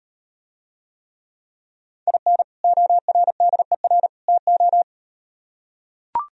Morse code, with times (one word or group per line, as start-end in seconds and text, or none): 2.1-2.4 IN
2.6-4.1 ORDER
4.3-4.8 TO